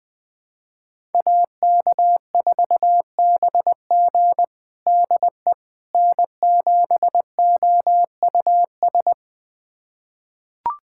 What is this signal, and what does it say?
Morse code, heard as a keyed tone, about 20 wpm, 700 Hz. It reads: AK4BG DE N7OUS
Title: USA Callsign DE Callsign - 5 Characters - 20wpm